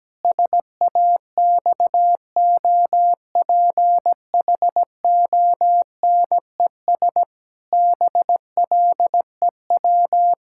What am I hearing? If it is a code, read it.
Morse code, heard as a keyed tone, about 17 wpm, 700 Hz. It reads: SAXOPHONES BLEW
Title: Pangram Challenge v2 - 17wpm